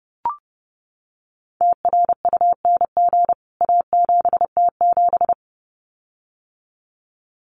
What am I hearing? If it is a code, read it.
Morse code, heard as a keyed tone, about 30 wpm, 700 Hz. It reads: TRUNG A7T7
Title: CWT Exchange 30wpm